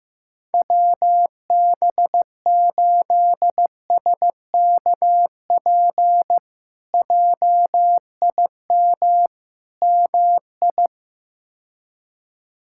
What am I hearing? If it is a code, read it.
Morse code, heard as a keyed tone, about 15 wpm, 700 Hz. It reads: WB8SKP JIM MI
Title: K1USN SST QSO Practice - 15wpm